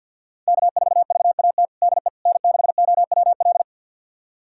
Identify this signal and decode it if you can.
Morse code, heard as a keyed tone, about 52 wpm, 700 Hz. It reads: K4VAT DE N6OWL
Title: USA Callsign DE Callsign - 5 Characters - 50wpm